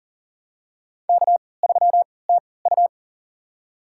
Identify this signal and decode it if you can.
Morse code, heard as a keyed tone, about 40 wpm, spaced as for 23 wpm, 700 Hz. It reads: K3TU
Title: USA Call Signs - 3x Character Spacing - 40wpm